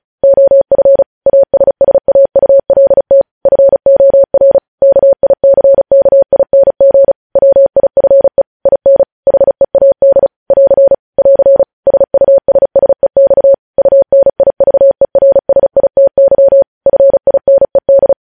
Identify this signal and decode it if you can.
Morse code, heard as a keyed tone, about 36 wpm, 550 Hz. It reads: OF ASSAULT FOR KICKING WIFE IN HEAD <AR> <AR> SUSSEX UNIVERSITY FINED
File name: podcast / News headlines-1x-35x35